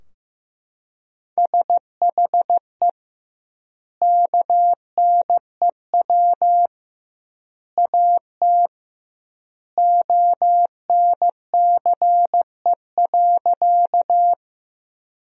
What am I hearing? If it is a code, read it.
Morse code, heard as a keyed tone, about 15 wpm, 700 Hz. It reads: SHE KNEW AT ONCE.
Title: 2x Word-Spacing - Sentences from Top 400 Words - 15wpm